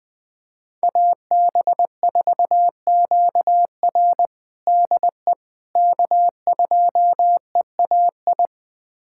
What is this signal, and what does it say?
Morse code, heard as a keyed tone, about 20 wpm, 700 Hz. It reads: AB4QR DE K2EAI
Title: USA Callsign DE Callsign - 5 Characters - 20wpm